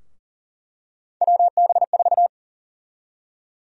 Morse code, heard as a keyed tone, about 40 wpm, 700 Hz. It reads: WB4
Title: USA Call Sign Prefixes - 40wpm